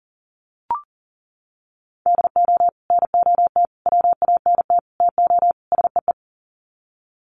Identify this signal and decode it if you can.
Morse code, heard as a keyed tone, about 40 wpm, 700 Hz. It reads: DO NOT WANT TO SEE
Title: Sets of 5 Words 40wpm